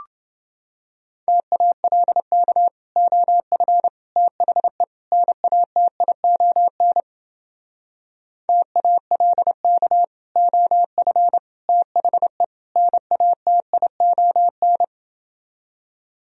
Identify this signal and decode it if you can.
Morse code, heard as a keyed tone, about 31 wpm, 700 Hz. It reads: TALK OF THE NATION TALK OF THE NATION